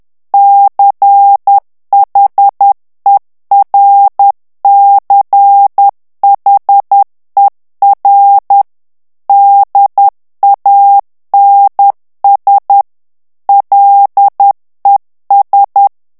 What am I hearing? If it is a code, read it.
Morse code, heard as a keyed tone, about 10 wpm, 800 Hz. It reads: CHERCHER DANS LES
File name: FRANCAIS TEXTE CLAIR 2 VITESSE 10 MOTS MINUTES